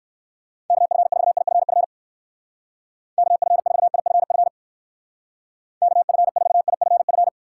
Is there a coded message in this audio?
Yes: Morse code from 0.7 to 7.3 s, heard as a keyed tone, about 57 wpm, 700 Hz. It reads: KF4IFF KF4IFF KF4IFF